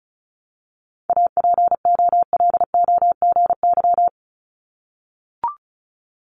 Morse code, heard as a keyed tone, about 36 wpm, 700 Hz. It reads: APOLOGY
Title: Multi-letter Phonemes 35wpm